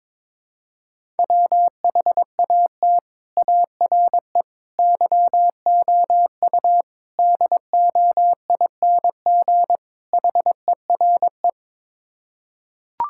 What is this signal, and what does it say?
Morse code, heard as a keyed tone, about 22 wpm, 700 Hz. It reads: WHAT ARE YOU DOING HERE
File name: practice / 22wpm / Build A Phrase - 5-Word Phrases - 22wpm